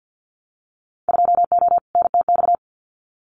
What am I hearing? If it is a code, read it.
Morse code, heard as a keyed tone, about 50 wpm, 700 Hz. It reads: 2O NTX
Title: Winter Field Day Exchanges 50wpm